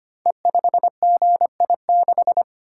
Morse code, heard as a keyed tone, about 25 wpm, 700 Hz. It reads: E5GI6
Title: Code Groups - 5 Letter:Numbers - 25wpm